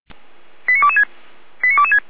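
Five-tone selective call (ZVEI), heard as quick runs of five tones, tones of about 70 ms. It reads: 89226 89226
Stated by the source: ZVEI five-tone selective call received on an SDR, recorded by feuerwehr-orlamuende (fire brigade Orlamünde).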